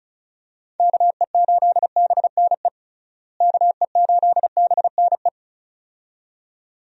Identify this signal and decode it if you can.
Morse code, heard as a keyed tone, about 36 wpm, 700 Hz. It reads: KE8BNE KE8BNE
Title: USA Call Signs - 6 Characters - Repeated 2x - 35wpm